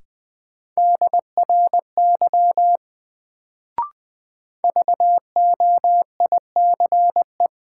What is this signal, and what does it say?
Morse code, heard as a keyed tone, about 20 wpm, 700 Hz. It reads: DRY VOICE
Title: Sentences from Top 500 Words - New Word Drill - 20wpm